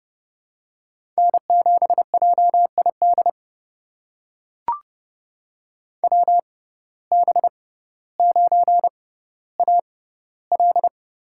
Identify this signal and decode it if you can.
Morse code, heard as a keyed tone, about 30 wpm, spaced as for 15 wpm, 700 Hz. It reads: N7JID WB9AL
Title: USA Call Signs - 6x Character Spacing - 30wpm